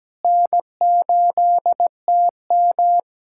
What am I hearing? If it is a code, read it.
Morse code, heard as a keyed tone, about 17 wpm, 700 Hz. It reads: N8TM
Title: SOTA QSO Practice 17wpm